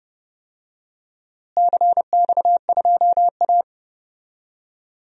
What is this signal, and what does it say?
Morse code, heard as a keyed tone, about 32 wpm, 700 Hz. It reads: CX2A
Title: DXCC-100 Call Signs - 4 Characters - Repeated 3x - 30wpm